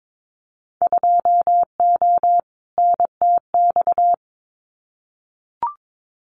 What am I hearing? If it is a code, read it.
Morse code, heard as a keyed tone, about 22 wpm, 700 Hz. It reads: 2O NTX
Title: Winter Field Day Exchanges 22wpm